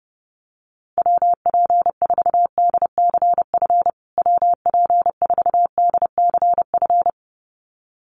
Morse code, heard as a keyed tone, about 30 wpm, 700 Hz. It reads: WP4DCF WP4DCF
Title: USA Call Signs Repeated 2x 30wpm